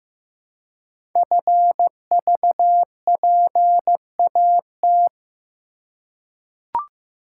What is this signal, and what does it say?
Morse code, heard as a keyed tone, about 15 wpm, 700 Hz. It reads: FVPAT
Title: Code Groups - 5 Letters - 15wpm